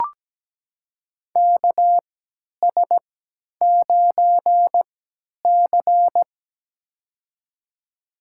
Morse code, heard as a keyed tone, about 17 wpm, spaced as for 10 wpm, 700 Hz. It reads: KS9C
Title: USA Call Signs - 4 Characters - 3x Character Spacing - 17wpm